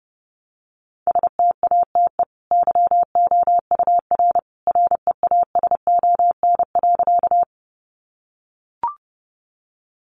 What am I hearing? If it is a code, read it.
Morse code, heard as a keyed tone, about 30 wpm, 700 Hz. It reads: STATE YOUR REASON.
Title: Sentences from Top 700 Words 30wpm